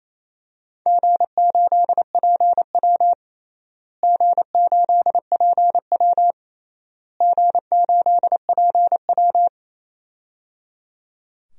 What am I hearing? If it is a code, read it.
Morse code, heard as a keyed tone, about 28 wpm, 700 Hz. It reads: G8PW G8PW G8PW
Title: DXCC-100 Call Signs - 4 Characters - Repeated 3x - 28wpm